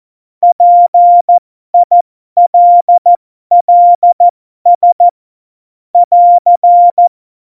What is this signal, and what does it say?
Morse code, heard as a keyed tone, about 14 wpm, spaced as for 12 wpm, 700 Hz. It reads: PILLS <AR>
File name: Morsekurs / Witze